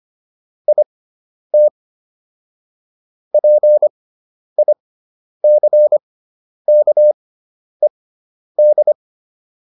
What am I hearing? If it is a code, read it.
Morse code, heard as a keyed tone, about 25 wpm, spaced as for 10 wpm, 600 Hz. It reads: IT PICKED